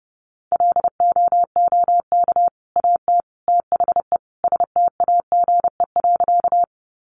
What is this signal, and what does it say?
Morse code, heard as a keyed tone, about 30 wpm, 700 Hz. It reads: LOOK AT THE STAGE.
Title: Sentences from Top 1200 Words 30wpm